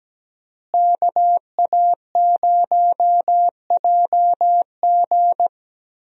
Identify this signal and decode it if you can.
Morse code, heard as a keyed tone, about 17 wpm, 700 Hz. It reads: KA0JG
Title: USA Call Signs - 5 Characters - Repeated 3x - 17wpm